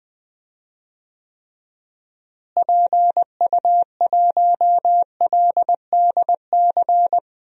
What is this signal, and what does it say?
Morse code, heard as a keyed tone, about 20 wpm, 700 Hz. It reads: PU1LDC
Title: DXCC-100 Call Signs - 6 Characters - Repeated 2x - 20wpm